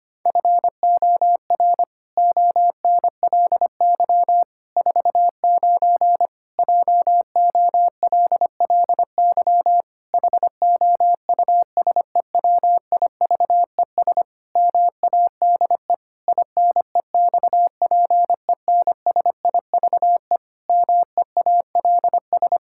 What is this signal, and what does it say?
Morse code, heard as a keyed tone, about 25 wpm, 700 Hz. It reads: FOR ONLY 49 JOLLY HOUSEWIVES MADE INEXPENSIVE MEALS
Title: Pangram Challenge v2 - 25wpm